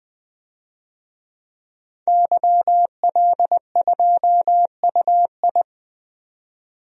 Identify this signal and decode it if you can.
Morse code, heard as a keyed tone, about 20 wpm, 700 Hz. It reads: YL2UI
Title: DXCC-100 Call Signs - 5 Characters - 20wpm